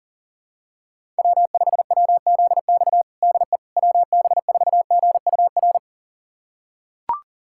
Morse code, heard as a keyed tone, about 40 wpm, 700 Hz. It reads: W5WZX DE WB4GUR
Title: USA Callsign DE Callsign 40wpm